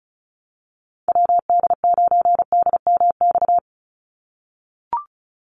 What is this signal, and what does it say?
Morse code, heard as a keyed tone, about 34 wpm, 700 Hz. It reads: WD9DMX
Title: USA Call Signs 35wpm